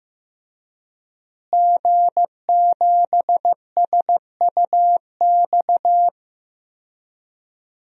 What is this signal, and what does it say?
Morse code, heard as a keyed tone, about 15 wpm, 700 Hz. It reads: G7SUX
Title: DXCC-100 Call Signs - 5 Characters - Repeated 2x - 15wpm